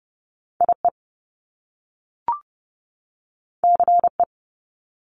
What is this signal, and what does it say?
Morse code, heard as a keyed tone, about 30 wpm, 700 Hz. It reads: IE CE
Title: Top Two Letter Combinations in Words 30wpm